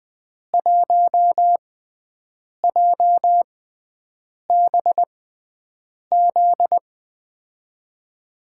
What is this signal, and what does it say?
Morse code, heard as a keyed tone, about 20 wpm, spaced as for 7 wpm, 700 Hz. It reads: 1JBZ
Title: USA Call Signs - 6 Characters - 6x Character Spacing - 20wpm